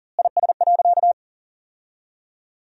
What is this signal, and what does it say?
Morse code, heard as a keyed tone, about 40 wpm, 700 Hz. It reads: IS.
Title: 4x Word-Spacing - Sentences from Top 200 Words - 40wpm